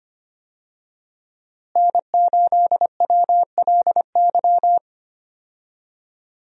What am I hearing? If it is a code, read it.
Morse code, heard as a keyed tone, about 25 wpm, 700 Hz. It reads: N8WLY
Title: USA Call Signs - 5 Characters - Repeated 3x - 25wpm